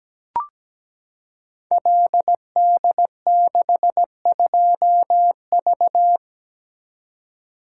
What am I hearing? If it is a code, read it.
Morse code, heard as a keyed tone, about 17 wpm, 700 Hz. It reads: LD62V
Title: Code Groups - 5 Letter:Numbers - 17wpm